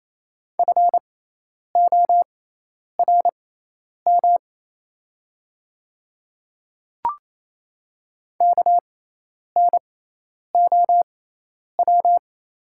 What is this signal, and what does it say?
Morse code, heard as a keyed tone, about 28 wpm, spaced as for 10 wpm, 700 Hz. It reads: FORM KNOW
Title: 6x Character Spacing - Top 200 Words - 28wpm